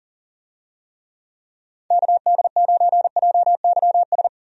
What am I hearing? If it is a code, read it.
Morse code, heard as a keyed tone, about 38 wpm, 700 Hz. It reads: KD9JYS
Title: USA Call Signs - 6 Characters - 40wpm